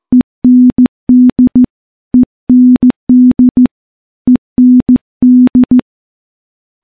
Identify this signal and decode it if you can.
Morse code, heard as a keyed tone, about 15 wpm, 250 Hz. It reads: END END END
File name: morse / morse-training-15WPM